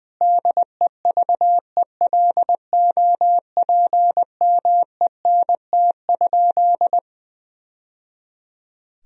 Morse code, heard as a keyed tone, about 20 wpm, 700 Hz. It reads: DEVELOPMENT?
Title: Sentences from Top 1000 Words - Review Part 4 - 20wpm